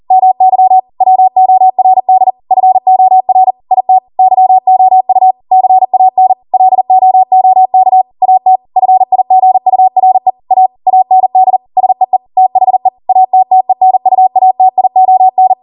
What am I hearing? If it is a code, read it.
Morse code, heard as a keyed tone, about 40 wpm, 750 Hz. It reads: MY WORD FOR IT YOU CAN LOOK AT FIGURE A AND SEE THE ATTENUATION